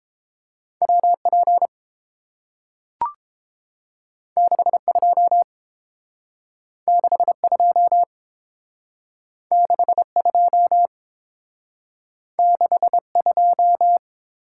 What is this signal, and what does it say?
Morse code, heard as a keyed tone, about 30 wpm, 700 Hz. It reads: WP 62 62 62 62